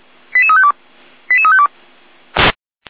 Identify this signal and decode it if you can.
Five-tone selective call (ZVEI), heard as quick runs of five tones, tones of about 70 ms. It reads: 88352 88352
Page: /455/SDR/ZVEI/88189/